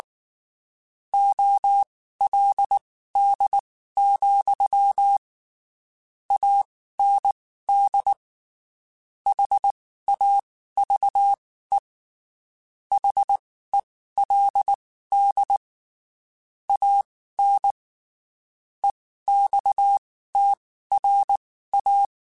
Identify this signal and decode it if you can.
Morse code, heard as a keyed tone, about 19 wpm, spaced as for 14 wpm, 800 Hz. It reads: OLD, AND HAVE HELD AN EXTRA
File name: (spaced) / QSOs Spaced-mp3s / kc7sac @18 sp